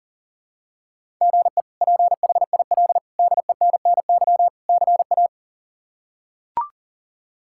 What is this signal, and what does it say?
Morse code, heard as a keyed tone, about 40 wpm, 690 Hz. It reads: GE PHIL DENNY CA
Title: K1USN SST QSO Practice - 40wpm